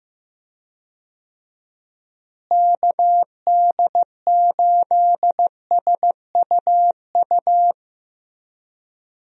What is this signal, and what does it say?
Morse code, heard as a keyed tone, about 15 wpm, 700 Hz. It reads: KD8SUU
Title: USA Call Signs - 6 Characters - Repeated 2x - 15wpm